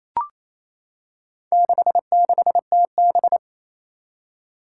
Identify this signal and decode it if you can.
Morse code, heard as a keyed tone, about 27 wpm, 700 Hz. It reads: 66TB
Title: Four Challenging Letter-Number Combinations 28wpm